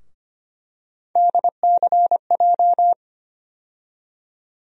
Morse code, heard as a keyed tone, about 25 wpm, 700 Hz. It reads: DCJ
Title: USA Call Sign Postfixes - 3 Letters 25wpm